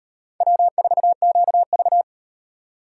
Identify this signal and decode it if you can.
Morse code, heard as a keyed tone, about 35 wpm, 700 Hz. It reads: W4QV